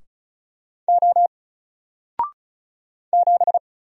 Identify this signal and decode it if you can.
Morse code, heard as a keyed tone, about 36 wpm, 700 Hz. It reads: O 7
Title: Single Letter-Number 35wpm